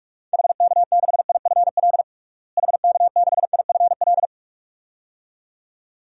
Morse code, heard as a keyed tone, about 47 wpm, 700 Hz. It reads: HK6IFL HK6IFL
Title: DXCC-100 Call Signs - 6 Characters - Repeated 2x - 45wpm